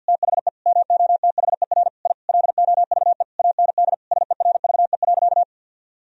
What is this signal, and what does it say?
Morse code, heard as a keyed tone, about 46 wpm, 700 Hz. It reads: THE MOTHER I LOVE AND SERVE.
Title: Sentences from Top 1000 Words - Review Part 2 - 50wpm